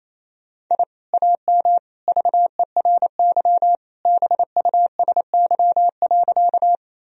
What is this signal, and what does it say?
Morse code, heard as a keyed tone, about 28 wpm, 700 Hz. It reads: I AM VERY BUSY.
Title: Sentences from Top 1000 Words - Review Part 3 - 28wpm